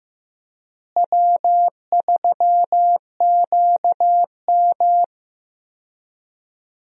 Morse code, heard as a keyed tone, about 15 wpm, 700 Hz. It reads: W3QM
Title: USA Call Signs - 4 Characters - Repeated 3x - 15wpm